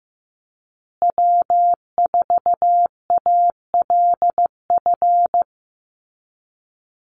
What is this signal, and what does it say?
Morse code, heard as a keyed tone, about 15 wpm, 700 Hz. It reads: W4ALF
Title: Call Signs Repeated 2x - 15wpm